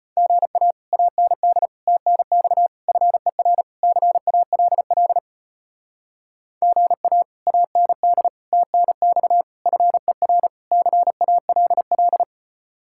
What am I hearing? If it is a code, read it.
Morse code, heard as a keyed tone, about 33 wpm, 700 Hz. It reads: GA AND TNX FER CALL GA AND TNX FER CALL